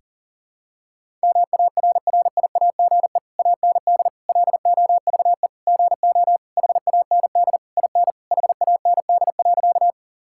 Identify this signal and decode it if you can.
Morse code, heard as a keyed tone, about 37 wpm, 700 Hz. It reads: MARRIAGE AND LOVE GO HAND IN HAND.
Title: Sentences from Top 1000 Words - Review Part 5 - 40wpm